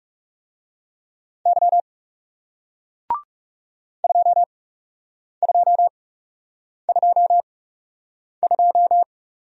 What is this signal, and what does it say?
Morse code, heard as a keyed tone, about 41 wpm, spaced as for 8 wpm, 700 Hz. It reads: Y 2222